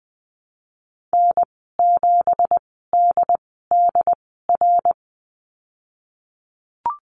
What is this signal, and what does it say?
Morse code, heard as a keyed tone, about 20 wpm, spaced as for 15 wpm, 700 Hz. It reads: N7DDR
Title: 179 Lesson 29g - Slash - Call Signs 20wpm